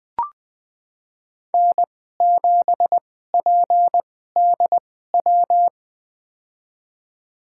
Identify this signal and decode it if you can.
Morse code, heard as a keyed tone, about 20 wpm, spaced as for 15 wpm, 700 Hz. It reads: N7PDW
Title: USA Call Signs - 2x Character Spacing - 20wpm